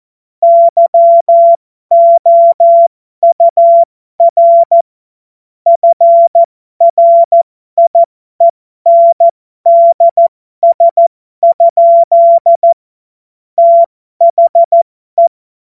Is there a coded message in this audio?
Yes: Morse code from 0.4 to 15.3 s, heard as a keyed tone, about 14 wpm, spaced as for 12 wpm, 680 Hz. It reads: YOUR FRIENDS? THE